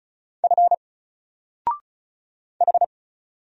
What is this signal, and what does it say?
Morse code, heard as a keyed tone, about 34 wpm, 700 Hz. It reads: F H
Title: Single Character (Letter:Number:Punctuation:Prosign) 35wpm